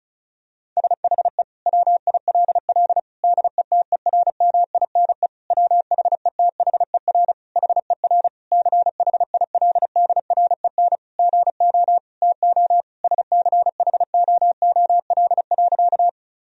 Morse code, heard as a keyed tone, about 34 wpm, 700 Hz. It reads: SHE WILL DETERMINE WHETHER HER CHILDREN GO TO SCHOOL.
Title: Sentences from Top 1000 Words - Review Part 4 - 35wpm